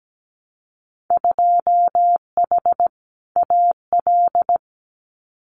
Morse code, heard as a keyed tone, about 17 wpm, 700 Hz. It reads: 2H AL
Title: Winter Field Day Exchanges 17wpm